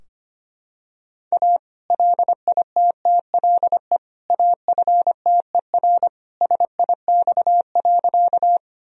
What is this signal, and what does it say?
Morse code, heard as a keyed tone, about 25 wpm, 700 Hz. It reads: A LITTLE AFTER SIX.
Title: Sentences from Top 1000 Words - Review Part 2 - 25wpm